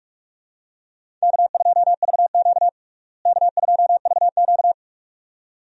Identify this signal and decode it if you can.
Morse code, heard as a keyed tone, about 45 wpm, 700 Hz. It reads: K2VQ K2VQ
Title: USA Call Signs - 4 Characters - Repeated 2x - 45wpm